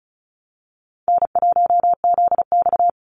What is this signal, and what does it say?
Morse code, heard as a keyed tone, about 34 wpm, 700 Hz. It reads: N1ZX
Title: Easy Call Signs Repeated 2x - 35wpm